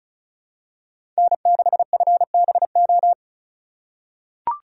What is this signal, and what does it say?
Morse code, heard as a keyed tone, about 36 wpm, 700 Hz. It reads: N6FBO
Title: USA Call Signs - 5 Characters - 2x Character Spacing - 35wpm